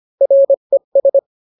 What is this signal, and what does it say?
Morse code, heard as a keyed tone, about 25 wpm, 550 Hz. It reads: RES